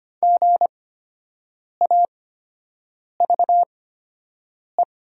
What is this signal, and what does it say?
Morse code, heard as a keyed tone, about 25 wpm, spaced as for 7 wpm, 700 Hz. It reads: GAVE
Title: 8x Character Spacing - Top 500 Words - 25wpm